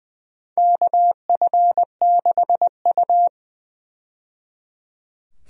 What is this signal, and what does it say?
Morse code, heard as a keyed tone, about 20 wpm, 700 Hz. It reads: KF6U
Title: USA Call Signs - 4 Characters - 20wpm